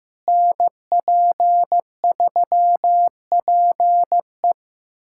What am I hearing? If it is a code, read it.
Morse code, heard as a keyed tone, about 16 wpm, 700 Hz. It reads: NP3PE
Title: USA Callsign DE Callsign - 5 Characters - 15wpm